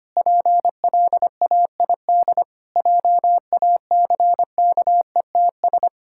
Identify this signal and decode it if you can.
Morse code, heard as a keyed tone, about 25 wpm, 700 Hz. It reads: PLAID JACKETS
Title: Pangram Challenge v2 - 25wpm